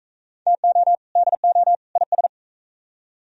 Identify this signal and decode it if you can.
Morse code, heard as a keyed tone, about 45 wpm, 700 Hz. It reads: TO DO IS